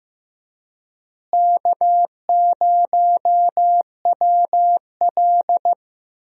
Morse code, heard as a keyed tone, about 16 wpm, 700 Hz. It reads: K0WL
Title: USA Call Signs - 4 Characters - Repeated 3x - 15wpm